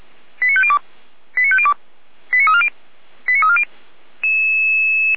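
Five-tone selective call (ZVEI), heard as quick runs of five tones, tones of about 70 ms. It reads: 89502 89502 89340 89340
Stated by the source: ZVEI five-tone selective call received on SDR.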